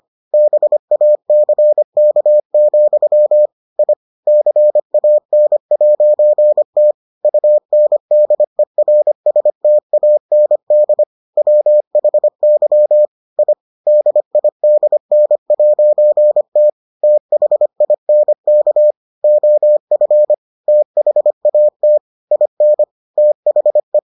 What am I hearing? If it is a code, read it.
Morse code, heard as a keyed tone, about 25 wpm, 600 Hz. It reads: BACK, I CAN'T UNDERSTAND WHY I DIDN'T THINK OF THAT IN THE